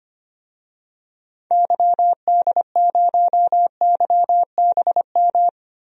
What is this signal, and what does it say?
Morse code, heard as a keyed tone, about 25 wpm, 700 Hz. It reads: YD0YBM
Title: DXCC-100 Call Signs - 6 Characters - Repeated 3x - 25wpm